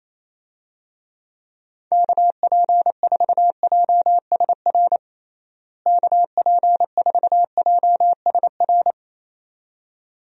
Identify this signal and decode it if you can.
Morse code, heard as a keyed tone, about 28 wpm, 700 Hz. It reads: KP4JSR KP4JSR
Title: USA Call Signs - 6 Characters - Repeated 2x - 28wpm